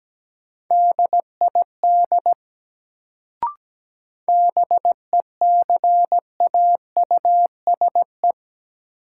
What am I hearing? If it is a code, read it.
Morse code, heard as a keyed tone, about 17 wpm, 700 Hz. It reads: DID BECAUSE
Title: Sets of 2 Words - New Word Drill - 17wpm